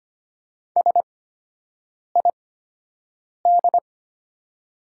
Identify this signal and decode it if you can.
Morse code, heard as a keyed tone, about 25 wpm, spaced as for 7 wpm, 700 Hz. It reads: SID
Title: 8x Character Spacing - Top 100 Words - 25wpm